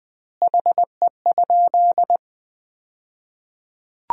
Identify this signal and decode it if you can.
Morse code, heard as a keyed tone, about 21 wpm, 700 Hz. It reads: HE?
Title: 2x Word-Spacing - Sentences from Top 300 Words - 20wpm